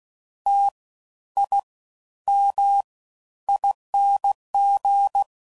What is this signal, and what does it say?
Morse code, heard as a keyed tone, about 16 wpm, spaced as for 11 wpm, 800 Hz. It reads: TIMING
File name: Suffixes / Special INGs / Fast Ings